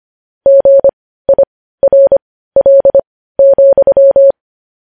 Morse code, heard as a keyed tone, about 25 wpm, spaced as for 15 wpm, 550 Hz. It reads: GIRL,
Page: news-headlines-in-morse-code-at-15-wpm